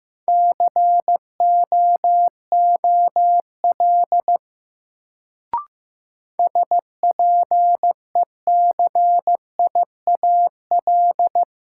Morse code, heard as a keyed tone, about 15 wpm, 700 Hz. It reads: COOL SPECIAL
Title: Sentences from Top 1200 Words - New Word Drill Alt - 15wpm